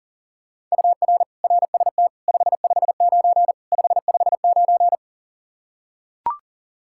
Morse code, heard as a keyed tone, about 40 wpm, 700 Hz. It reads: UR RST 559 559
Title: Practice QSO - Exchange 1 - Beginning - 40wpm